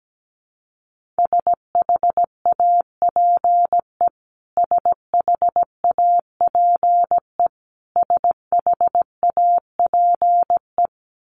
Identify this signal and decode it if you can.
Morse code, heard as a keyed tone, about 17 wpm, 700 Hz. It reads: SHAPE SHAPE SHAPE
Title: Top 500 Words Repeated 17wpm